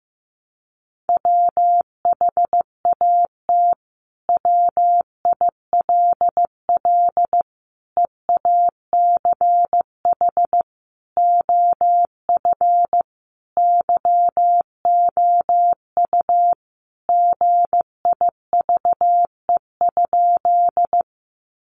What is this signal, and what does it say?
Morse code, heard as a keyed tone, about 15 wpm, 700 Hz. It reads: WHAT WILL EACH OF YOU GIVE?
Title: Sentences from Top 200 Words 15wpm